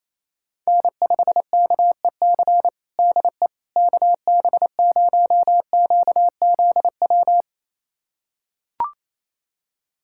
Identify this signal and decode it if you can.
Morse code, heard as a keyed tone, about 28 wpm, 700 Hz. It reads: N5KEC DE KB0QZW
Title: USA Callsign DE Callsign 28wpm